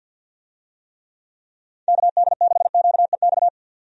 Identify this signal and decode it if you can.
Morse code, heard as a keyed tone, about 49 wpm, 700 Hz. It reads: KD6QEX
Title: USA Call Signs - 6 Characters - 50wpm